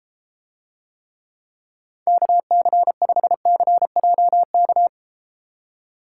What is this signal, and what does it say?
Morse code, heard as a keyed tone, about 32 wpm, 700 Hz. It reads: KC5CJK